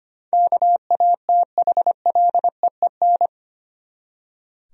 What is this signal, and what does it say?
Morse code, heard as a keyed tone, about 25 wpm, 700 Hz. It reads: KATHLEEN
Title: USA Common Names - 25wpm